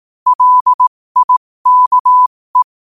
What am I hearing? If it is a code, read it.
Morse code, heard as a keyed tone, about 18 wpm, spaced as for 15 wpm, 1000 Hz. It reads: LIKE
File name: qotdfiles / 15wpm